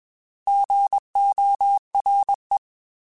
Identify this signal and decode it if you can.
Morse code, heard as a keyed tone, about 20 wpm, 800 Hz. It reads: GORE
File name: Alphabetical / Alphabetical mp3s / Gs @20